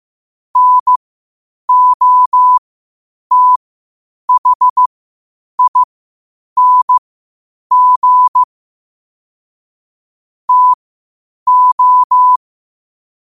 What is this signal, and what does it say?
Morse code, heard as a keyed tone, about 15 wpm, spaced as for 8 wpm, 1000 Hz. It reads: NOTHING TO
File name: qotdfiles / 8wpm